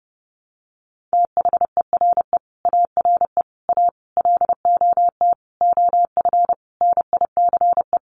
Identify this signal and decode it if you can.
Morse code, heard as a keyed tone, about 30 wpm, 700 Hz. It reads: THERE ARE A LOT OF NICE
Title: Sentences from Top 1100 Words 30wpm